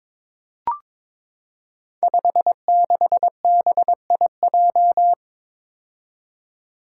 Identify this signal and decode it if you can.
Morse code, heard as a keyed tone, about 22 wpm, 700 Hz. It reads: 56BIJ
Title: Five Challenging Letter-Number Combinations 22wpm